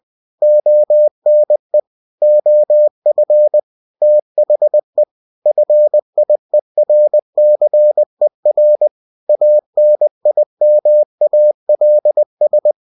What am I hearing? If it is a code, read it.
Morse code, heard as a keyed tone, about 20 wpm, 600 Hz. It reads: ONE OF THE FIERCER ANIMALS
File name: cw-books / winnie-the-pooh / winnie-20wpm-page-0062